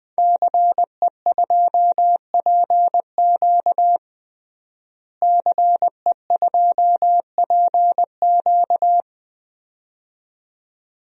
Morse code, heard as a keyed tone, about 21 wpm, 700 Hz. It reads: CE2PQ CE2PQ
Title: DXCC-100 Call Signs - 5 Characters - Repeated 2x - 20wpm